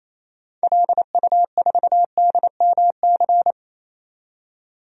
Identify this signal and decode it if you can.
Morse code, heard as a keyed tone, about 28 wpm, 700 Hz. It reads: LU4DMC
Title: DXCC-100 Call Signs - 6 Characters - 28wpm